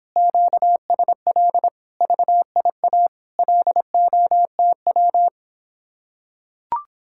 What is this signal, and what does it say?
Morse code, heard as a keyed tone, about 26 wpm, 700 Hz. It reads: QSL VIA LOTW